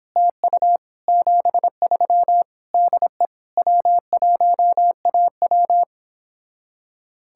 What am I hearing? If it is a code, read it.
Morse code, heard as a keyed tone, about 31 wpm, 700 Hz. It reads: TU 73 DE W1AW